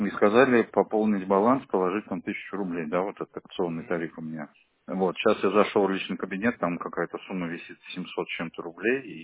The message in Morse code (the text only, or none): none